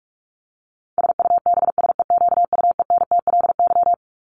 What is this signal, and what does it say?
Morse code, heard as a keyed tone, about 44 wpm, 700 Hz. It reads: SUBSEQUENTLY
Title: LY Suffix - Top 100 Words with Base Words 45wpm